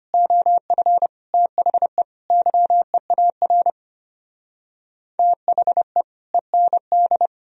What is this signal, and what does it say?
Morse code, heard as a keyed tone, about 30 wpm, 700 Hz. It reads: OF THE YEAR THE END
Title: Speed-Racing - Sets of 5 Words 25wpm